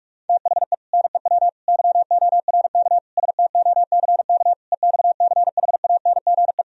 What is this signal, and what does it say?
Morse code, heard as a keyed tone, about 50 wpm, 700 Hz. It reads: THE NEW YORK STOCK EXCHANGE